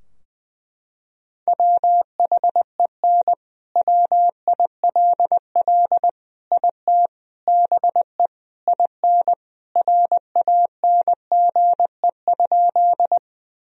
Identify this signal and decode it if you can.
Morse code, heard as a keyed tone, about 20 wpm, 700 Hz. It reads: WHEN WILL IT BE IN RANGE?
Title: Sentences from Top 1000 Words - Review Part 4 - 20wpm